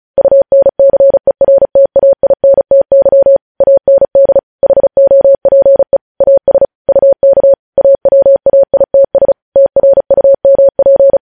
Morse code, heard as a keyed tone, about 36 wpm, 550 Hz. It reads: UNCERTAINTY AND HOPE AS UK AWAITS TRUMP
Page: News Headlines in Morse Code at 35 WPM